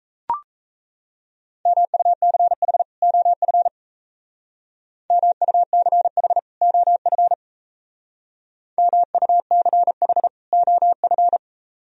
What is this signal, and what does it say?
Morse code, heard as a keyed tone, about 37 wpm, 700 Hz. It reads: MUCH OF MUCH OF MUCH OF